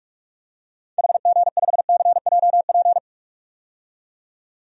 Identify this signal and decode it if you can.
Morse code, heard as a keyed tone, about 42 wpm, 700 Hz. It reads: HG5CJP
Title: DXCC-100 Call Signs - 6 Characters - 45wpm